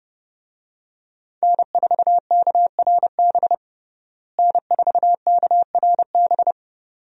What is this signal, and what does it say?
Morse code, heard as a keyed tone, about 29 wpm, 700 Hz. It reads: N4KRB N4KRB
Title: USA Call Signs - 5 Characters - Repeated 2x - 30wpm